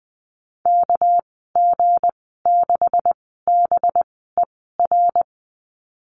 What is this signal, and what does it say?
Morse code, heard as a keyed tone, about 20 wpm, spaced as for 15 wpm, 700 Hz. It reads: KG6BER
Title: 238 Lesson 38g - Letter q - Call Signs 20wpm